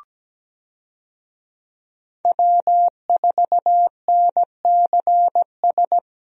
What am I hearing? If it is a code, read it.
Morse code, heard as a keyed tone, about 17 wpm, 700 Hz. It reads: W4NCS
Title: USA Call Signs - 5 Characters - Repeated 3x - 17wpm